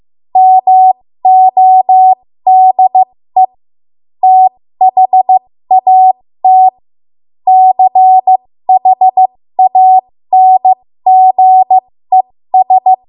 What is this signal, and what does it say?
Morse code, heard as a keyed tone, about 15 wpm, spaced as for 13 wpm, 750 Hz. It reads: MODE THAT CHANGES
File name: Morse / 13WPM